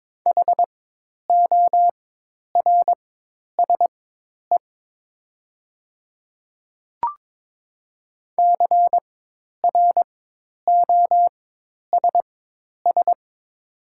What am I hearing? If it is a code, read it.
Morse code, heard as a keyed tone, about 22 wpm, spaced as for 10 wpm, 700 Hz. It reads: HORSE CROSS
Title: 4x Character Spacing - Top 300 Words - 22wpm